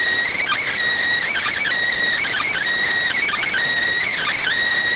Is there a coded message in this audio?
Yes: Emergency Alert System (SAME) tones (header bursts) from 0.3 to 4.5 s, heard as harsh buzzing bursts.